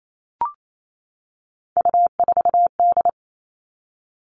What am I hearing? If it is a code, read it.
Morse code, heard as a keyed tone, about 28 wpm, 700 Hz. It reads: U4D
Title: All Cut Numbers - 3-Digit 28wpm